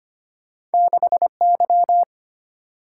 Morse code, heard as a keyed tone, about 34 wpm, spaced as for 30 wpm, 700 Hz. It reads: 6Y
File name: practice / 25wpm / Speed-Racing - Two Letter-Number Combinations - 25wpm